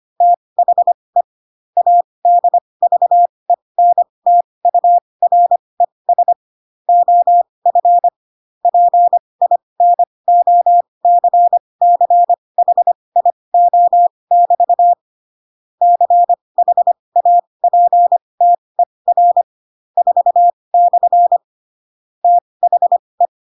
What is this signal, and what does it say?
Morse code, heard as a keyed tone, about 25 wpm, spaced as for 20 wpm, 700 Hz. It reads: THE ADVENTURES OF PINOCCHIO<BT> CHAPTER 4/ THE
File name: Pinocchio / Pinocchio ch4 15 25 ik1hge